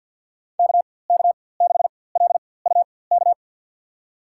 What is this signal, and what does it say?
Morse code, heard as a keyed tone, about 48 wpm, spaced as for 23 wpm, 700 Hz. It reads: KK6LUK
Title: USA Call Signs - 4x Character Spacing - 50wpm